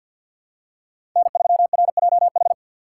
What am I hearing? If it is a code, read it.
Morse code, heard as a keyed tone, about 52 wpm, 690 Hz. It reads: N3RJH
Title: USA Call Signs - 3x Character Spacing - 50wpm